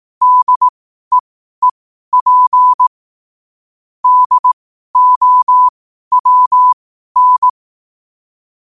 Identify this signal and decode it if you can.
Morse code, heard as a keyed tone, about 18 wpm, spaced as for 12 wpm, 1000 Hz. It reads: DEEP DOWN
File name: qotdfiles / 12wpm